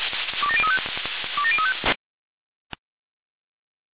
Five-tone selective call (ZVEI), heard as quick runs of five tones, tones of about 70 ms. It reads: 27035 27035